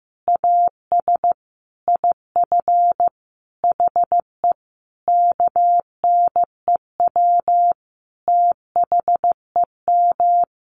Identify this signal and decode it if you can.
Morse code, heard as a keyed tone, about 15 wpm, 700 Hz. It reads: AS IF HE KNEW THEM
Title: Sentences from Top 1400 Words 15wpm